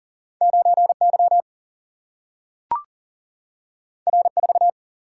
Morse code, heard as a keyed tone, about 39 wpm, 700 Hz. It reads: 9Y R4
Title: Single-Letter Single-Number 40wpm